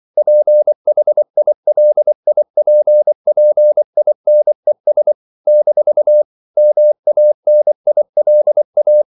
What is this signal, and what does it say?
Morse code, heard as a keyed tone, about 24 wpm, 600 Hz. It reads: PHILIPPINES - MANILA
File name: CW / Practice / NationalCapitals24